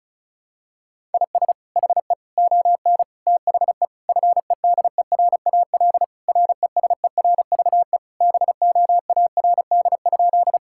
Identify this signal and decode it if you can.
Morse code, heard as a keyed tone, about 36 wpm, 700 Hz. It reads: IS HE ON THE FEDERAL RESERVE BOARD?
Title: Sentences from Top 1000 Words - Review Part 6 - 35wpm